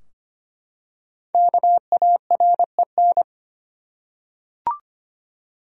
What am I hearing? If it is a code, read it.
Morse code, heard as a keyed tone, about 25 wpm, 700 Hz. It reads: KAREN
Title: USA Common Names - 25wpm